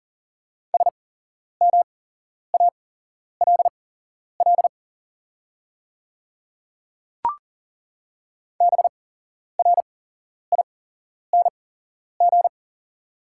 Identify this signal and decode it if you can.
Morse code, heard as a keyed tone, about 40 wpm, spaced as for 11 wpm, 700 Hz. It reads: SMALL BRING
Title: 8x Character Spacing - Top 500 Words - 40wpm